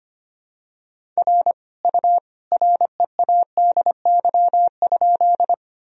Morse code, heard as a keyed tone, about 25 wpm, 700 Hz. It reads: R U READY?
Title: Sentences from Top 400 Words and Abbreviations 25wpm